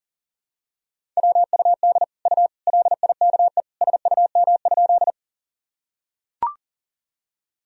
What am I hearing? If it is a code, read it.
Morse code, heard as a keyed tone, about 37 wpm, 700 Hz. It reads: WUD U LIKE SUM?
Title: Sentences from Top 100 Words and Abbreviations 40wpm